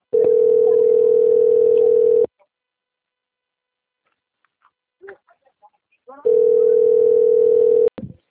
Ringback tone (North America), 440 + 480 Hz.